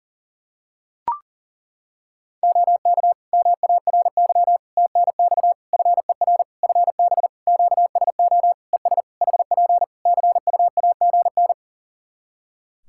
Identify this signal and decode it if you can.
Morse code, heard as a keyed tone, about 40 wpm, 700 Hz. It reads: OK MARY TNX FER FB QSO ES HP CUAGN
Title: Practice QSO - Exchange 4 - Ending - 40wpm